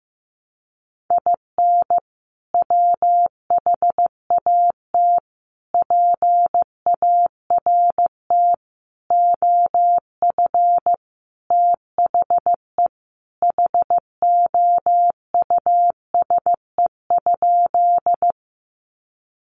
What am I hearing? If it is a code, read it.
Morse code, heard as a keyed tone, about 15 wpm, 700 Hz. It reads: IN WHAT PART OF THE HOUSE?
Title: Sentences from Top 200 Words 15wpm